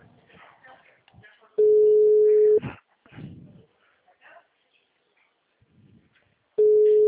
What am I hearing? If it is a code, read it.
Ringback tone (Europe, 425 Hz), with speech.
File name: out- -2006- - - .15742